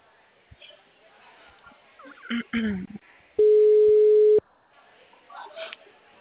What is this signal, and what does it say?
Ringback tone (Europe, 425 Hz), with speech.